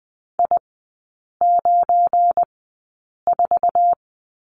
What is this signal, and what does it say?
Morse code, heard as a keyed tone, about 20 wpm, spaced as for 8 wpm, 690 Hz. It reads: I94
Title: 212 Lesson 34e - Number 8 - Three Characters 20wpm